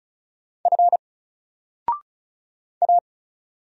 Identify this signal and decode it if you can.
Morse code, heard as a keyed tone, about 36 wpm, 700 Hz. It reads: F A
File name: Single-Letter-Number-35wpm